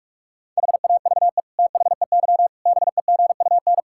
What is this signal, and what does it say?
Morse code, heard as a keyed tone, about 46 wpm, 700 Hz. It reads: HAVE THEY BEGUN
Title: Sentences from Top 1000 Words - Review Part 6 - 45wpm